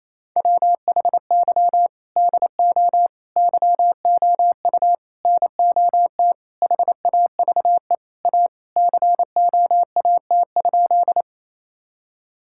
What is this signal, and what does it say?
Morse code, heard as a keyed tone, about 28 wpm, 700 Hz. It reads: WHY DO YOU NOT HAVE A COAT?
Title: Sentences from Top 1600 Words 28wpm